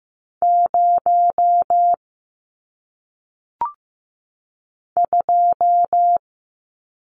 Morse code, heard as a keyed tone, about 15 wpm, 700 Hz. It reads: 0 2
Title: Numbers - 1-Digit 15wpm